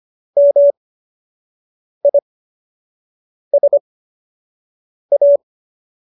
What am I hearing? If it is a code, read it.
Morse code, heard as a keyed tone, about 25 wpm, spaced as for 6 wpm, 570 Hz. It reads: MISA